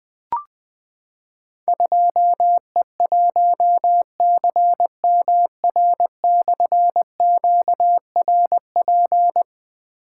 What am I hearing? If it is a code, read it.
Morse code, heard as a keyed tone, about 20 wpm, 700 Hz. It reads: 2E1CMR/QRP
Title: Difficult Call Signs 20wpm